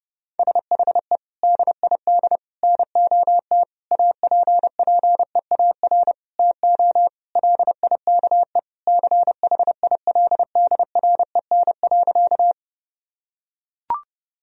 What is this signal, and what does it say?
Morse code, heard as a keyed tone, about 32 wpm, 700 Hz. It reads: SHE DID NOT APPEAR TO LIKE CHILDREN.
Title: Sentences from Top 1000 Words - Review Part 2 - 30wpm